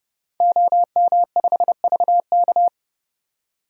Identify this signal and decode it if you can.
Morse code, heard as a keyed tone, about 30 wpm, 700 Hz. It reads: OM5VK
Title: DXCC-100 Call Signs - 5 Characters - 30wpm